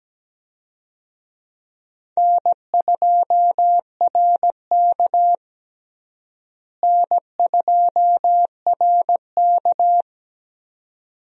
Morse code, heard as a keyed tone, about 17 wpm, 700 Hz. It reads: N2RK N2RK
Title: USA Call Signs - 4 Characters - Repeated 2x - 17wpm